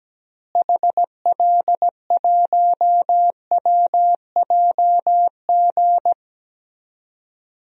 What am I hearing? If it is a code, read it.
Morse code, heard as a keyed tone, about 17 wpm, 700 Hz. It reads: HL1WJG
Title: DXCC-100 Call Signs - 6 Characters - Repeated 2x - 17wpm